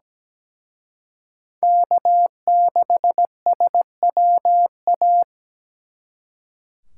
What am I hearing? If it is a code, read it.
Morse code, heard as a keyed tone, about 20 wpm, 700 Hz. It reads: K6SWA
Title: Speed Racing - USA 1x3 Call Signs 17wpm